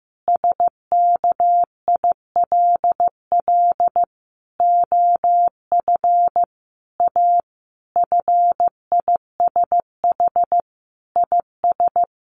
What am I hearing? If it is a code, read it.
Morse code, heard as a keyed tone, about 15 wpm, 700 Hz. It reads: SKILL OF A FISH IS
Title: Sentences from Top 1400 Words 15wpm